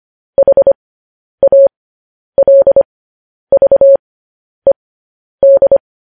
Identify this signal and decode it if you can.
Morse code, heard as a keyed tone, about 25 wpm, spaced as for 10 wpm, 550 Hz. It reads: HALVED
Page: news-headlines-in-morse-code-at-10-wpm